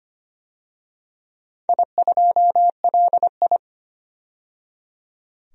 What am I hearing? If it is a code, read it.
Morse code, heard as a keyed tone, about 25 wpm, 700 Hz. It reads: I2LI
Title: DXCC-100 Call Signs - 4 Characters - 25wpm